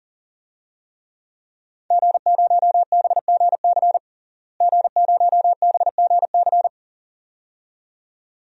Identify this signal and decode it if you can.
Morse code, heard as a keyed tone, about 40 wpm, 700 Hz. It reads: G0BGC G0BGC
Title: DXCC-100 Call Signs - 5 Characters - Repeated 2x - 40wpm